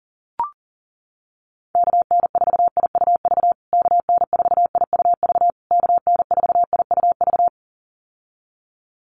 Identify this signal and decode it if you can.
Morse code, heard as a keyed tone, about 40 wpm, 700 Hz. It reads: KN4IUV KN4IUV KN4IUV
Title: USA Call Signs Repeated 3x 40wpm